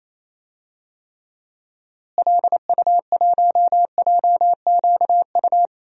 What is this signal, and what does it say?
Morse code, heard as a keyed tone, about 28 wpm, 700 Hz. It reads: LU1JQU
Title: DXCC-100 Call Signs - 6 Characters - Repeated 3x - 28wpm